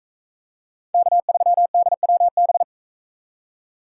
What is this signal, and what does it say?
Morse code, heard as a keyed tone, about 44 wpm, 700 Hz. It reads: K3DWB